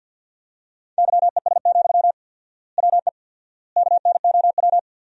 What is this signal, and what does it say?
Morse code, heard as a keyed tone, about 50 wpm, 700 Hz. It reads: YES, WE KNOW
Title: 4x Word-Spacing - Sentences from Top 500 Words - 50wpm